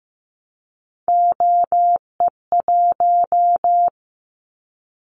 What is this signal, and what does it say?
Morse code, heard as a keyed tone, about 15 wpm, 700 Hz. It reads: OE1
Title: Callsign Prefixes for Top 25 DXCC - 15wpm